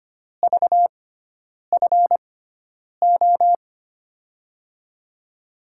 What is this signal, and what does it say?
Morse code, heard as a keyed tone, about 25 wpm, spaced as for 9 wpm, 700 Hz. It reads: VFO
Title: USA Call Signs - 6 Characters - 6x Character Spacing - 25wpm